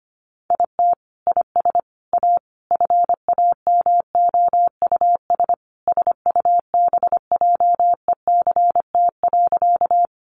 Morse code, heard as a keyed tone, about 25 wpm, 700 Hz. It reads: IT IS A FAMOUS SUBJECT.
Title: Sentences from Top 1500 Words 25wpm